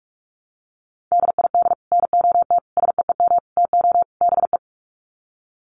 Morse code, heard as a keyed tone, about 44 wpm, 700 Hz. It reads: DID NOT SEEM TO BE
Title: Sets of 5 Words Encore 45wpm